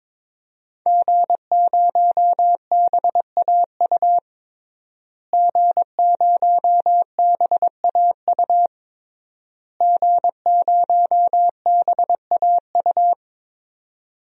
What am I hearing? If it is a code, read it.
Morse code, heard as a keyed tone, about 22 wpm, 700 Hz. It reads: G0BAU G0BAU G0BAU
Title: DXCC-100 Call Signs - 5 Characters - Repeated 3x - 22wpm